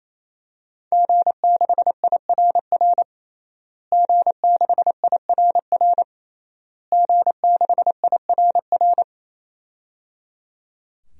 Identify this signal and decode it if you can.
Morse code, heard as a keyed tone, about 27 wpm, 700 Hz. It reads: G6IRR G6IRR G6IRR
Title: DXCC-100 Call Signs - 5 Characters - Repeated 3x - 28wpm